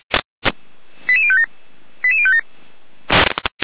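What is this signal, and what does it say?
Five-tone selective call (ZVEI), heard as quick runs of five tones, tones of about 80 ms. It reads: 88057 88057
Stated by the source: ZVEI five-tone selective call 88102, received by SDR, recorded by the Orlamünde fire brigade (feuerwehr-orlamuende).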